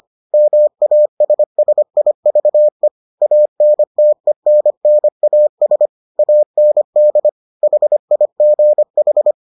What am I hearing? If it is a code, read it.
Morse code, heard as a keyed tone, about 25 wpm, 600 Hz. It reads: MASSIVE ANTENNAS AND HIGH